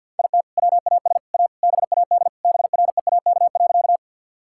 Morse code, heard as a keyed tone, about 52 wpm, 700 Hz. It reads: IT WAS A BAD BREAK.
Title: Sentences from Top 1000 Words - Review Part 6 - 50wpm